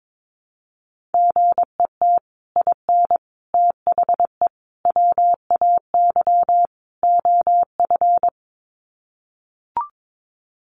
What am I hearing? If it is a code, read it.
Morse code, heard as a keyed tone, about 22 wpm, 700 Hz. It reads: GET IN THE WAY OF
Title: Sets of 5 Words Encore 22wpm